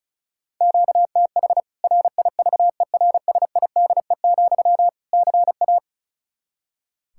Morse code, heard as a keyed tone, about 34 wpm, 700 Hz. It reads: QTH RIVERSIDE, CA
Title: K1USN SST QSO Practice 35wpm